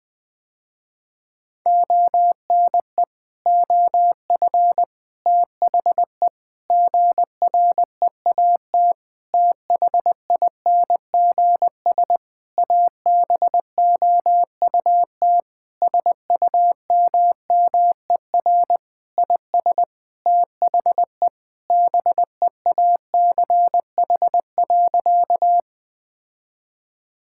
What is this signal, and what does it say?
Morse code, heard as a keyed tone, about 20 wpm, 700 Hz. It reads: ONE OF THE GREAT THINGS ABOUT SUMMER IS THE BEACH.
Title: Sentences with 6-Word Phrases - Spoken Beginning - 20wpm